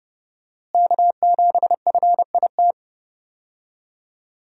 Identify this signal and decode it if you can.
Morse code, heard as a keyed tone, about 30 wpm, 700 Hz. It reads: K7FIT
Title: USA Call Signs - 5 Characters - Repeated 3x - 30wpm